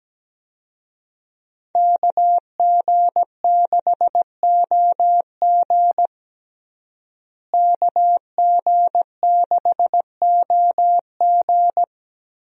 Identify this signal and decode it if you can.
Morse code, heard as a keyed tone, about 17 wpm, 700 Hz. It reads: KG6OG KG6OG
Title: USA Call Signs - 5 Characters - Repeated 2x - 17wpm